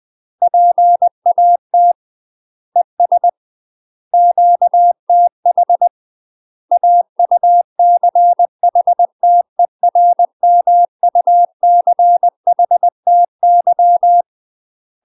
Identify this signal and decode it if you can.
Morse code, heard as a keyed tone, about 20 wpm, 700 Hz. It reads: PAT ES QTH AUCHTERMUCHTY
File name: UK MASTER QSOs TRAINING / 20WPM / UK MASTER QSOs 0046